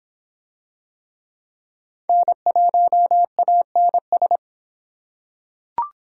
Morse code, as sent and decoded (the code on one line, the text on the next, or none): -. .---- .- -. ...
N1ANS